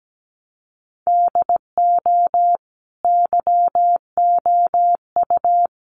Morse code, heard as a keyed tone, about 17 wpm, 700 Hz. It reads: DO YOU
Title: Sentences from Top 100 Words 17wpm